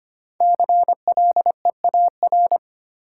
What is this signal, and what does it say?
Morse code, heard as a keyed tone, about 25 wpm, 700 Hz. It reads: CLEAR
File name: 5 Letter Words - First 3 Letters Given - 25wpm